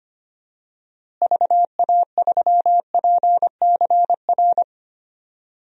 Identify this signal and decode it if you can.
Morse code, heard as a keyed tone, about 25 wpm, 700 Hz. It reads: VA3PCR
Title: DXCC-100 Call Signs - 6 Characters - Repeated 3x - 25wpm